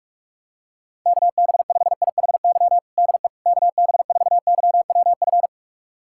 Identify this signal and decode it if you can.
Morse code, heard as a keyed tone, about 46 wpm, 700 Hz. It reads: KB5IHY DE KB4YWF
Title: USA Callsign DE Callsign - 6 Characters - 45wpm